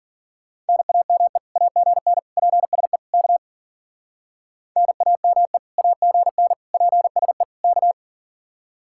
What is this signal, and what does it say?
Morse code, heard as a keyed tone, about 48 wpm, 700 Hz. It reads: NAME AGN PSE K NAME AGN PSE K
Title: Speed-Racing Common QSO Phrases - 40wpm